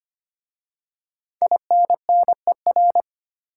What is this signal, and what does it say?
Morse code, heard as a keyed tone, about 25 wpm, 700 Hz. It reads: INNER
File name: 5 Letter Words - First 3 Letters Given - 25wpm